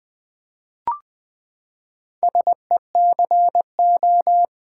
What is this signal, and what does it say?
Morse code, heard as a keyed tone, about 20 wpm, 700 Hz. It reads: SECO
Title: Build A Word - 4 Letters At A Time - 8-Letter Words - 20wpm